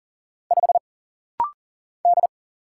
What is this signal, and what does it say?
Morse code, heard as a keyed tone, about 39 wpm, 700 Hz. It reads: 5 D
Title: Single Letter-Number - Rapid-Fire 40wpm